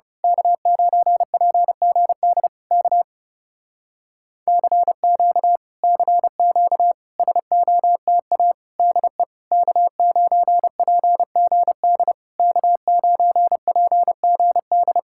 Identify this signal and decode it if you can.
Morse code, heard as a keyed tone, about 29 wpm, 700 Hz. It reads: K9PGD K CQ CQ SOTA DE K9PGD K9PGD
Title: Speed-Racing Common QSO Phrases - 30wpm